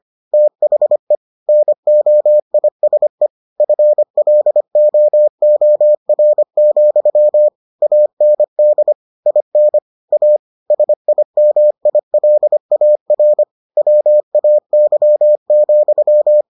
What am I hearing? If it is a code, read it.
Morse code, heard as a keyed tone, about 25 wpm, 600 Hz. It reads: THE NOISE FLOOR, AND IN A SIMILAR WAY,